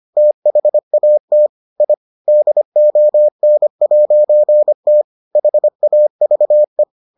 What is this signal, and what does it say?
Morse code, heard as a keyed tone, about 25 wpm, 600 Hz. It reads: THAT I DON'T HAVE